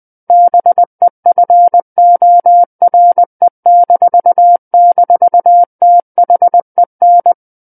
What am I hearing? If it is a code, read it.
Morse code, heard as a keyed tone, about 20 wpm, 700 Hz. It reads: BEFORE--THEN
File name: 20 WPM